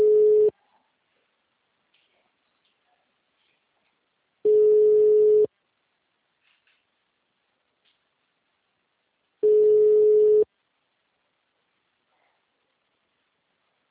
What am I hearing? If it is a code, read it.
Ringback tone (Europe, 425 Hz).
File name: out- -2022- - - .7417